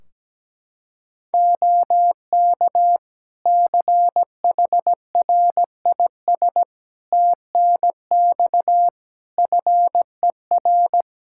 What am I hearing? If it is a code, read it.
Morse code, heard as a keyed tone, about 17 wpm, 700 Hz. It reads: OK CHRIS TNX FER
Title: Practice QSO - Exchange 4 - Ending - 17wpm